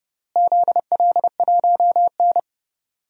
Morse code, heard as a keyed tone, about 32 wpm, 700 Hz. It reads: ZL1N
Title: DXCC-100 Call Signs - 4 Characters - Repeated 3x - 30wpm